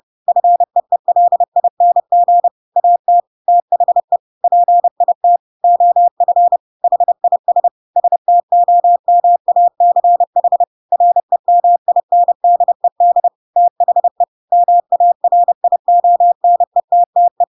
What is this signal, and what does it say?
Morse code, heard as a keyed tone, about 30 wpm, 700 Hz. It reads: FEELING AT THE PIT OF HIS STOMACH REMINDED THE MARIONETTE